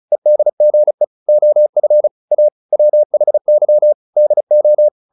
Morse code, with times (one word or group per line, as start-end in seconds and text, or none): 0.1-1.0 EDGE
1.3-2.1 OF
2.3-2.5 A
2.7-3.9 WHY
4.2-4.9 DO